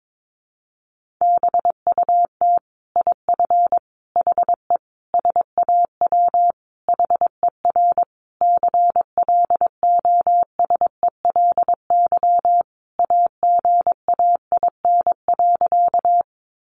Morse code, heard as a keyed tone, about 22 wpm, 700 Hz. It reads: BUT IF HE SAW HER CLOSELY AGAIN.
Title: Sentences from Top 1400 Words 22wpm